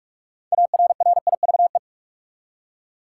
Morse code, heard as a keyed tone, about 45 wpm, 700 Hz. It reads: ARRIVE
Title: Six Letter Words - 45wpm